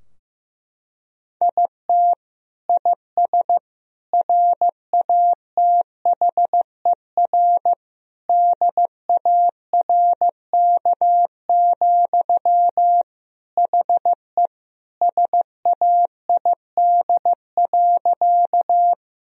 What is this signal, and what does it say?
Morse code, heard as a keyed tone, about 16 wpm, 700 Hz. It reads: IT IS RATHER DARK, HE SAID.
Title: Sentences from Top 1000 Words - Review Part 4 - 15wpm